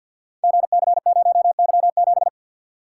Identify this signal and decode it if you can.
Morse code, heard as a keyed tone, about 53 wpm, 700 Hz. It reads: GC0Y7
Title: Code Groups - 5 Letter:Numbers - 50wpm